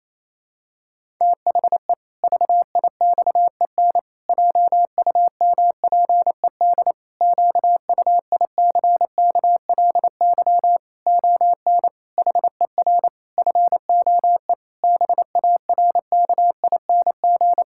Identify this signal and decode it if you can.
Morse code, heard as a keyed tone, about 28 wpm, 700 Hz. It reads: THE VIXEN JUMPED QUICKLY ON HER FOE BARKING
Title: Pangram Challenge v2 - 28wpm